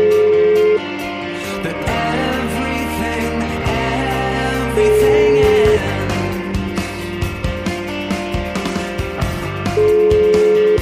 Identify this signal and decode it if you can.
Ringback tone (Europe, 425 Hz), with noise.